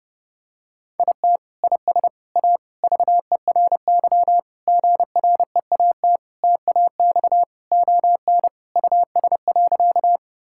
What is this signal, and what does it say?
Morse code, heard as a keyed tone, about 32 wpm, 700 Hz. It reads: IT IS A VERY GREAT TAX ON US.
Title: Sentences from Top 1000 Words - Review Part 5 - 30wpm